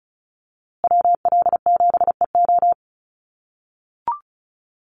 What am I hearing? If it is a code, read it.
Morse code, heard as a keyed tone, about 34 wpm, 700 Hz. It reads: WL7EO
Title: USA Call Signs Repeated 3x 35wpm